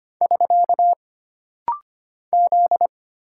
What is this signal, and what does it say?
Morse code, heard as a keyed tone, about 25 wpm, 700 Hz. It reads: <SK> Z
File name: Single Character (Letter:Number:Punctuation:Prosign) - Rapid-Fire - 25wpm